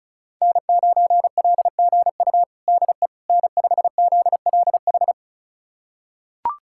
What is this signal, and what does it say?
Morse code, heard as a keyed tone, about 36 wpm, 700 Hz. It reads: N9LGU DE N5ZLH
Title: USA Callsign DE Callsign - 5 Characters - 35wpm